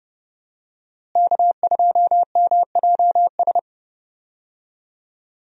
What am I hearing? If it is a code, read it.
Morse code, heard as a keyed tone, about 31 wpm, 700 Hz. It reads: K2MJS
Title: USA Call Signs - 5 Characters - Repeated 3x - 30wpm